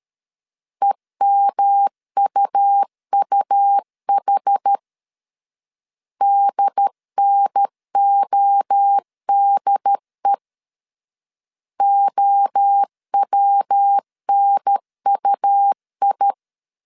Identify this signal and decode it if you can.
Morse code, heard as a keyed tone, about 13 wpm, 800 Hz. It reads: EMUUH DNODE OWNUI
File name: CW / DET / 1 3 1500C S B 12wpm 2828